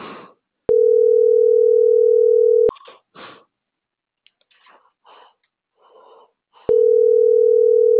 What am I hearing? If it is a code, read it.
Ringback tone (North America), 440 + 480 Hz.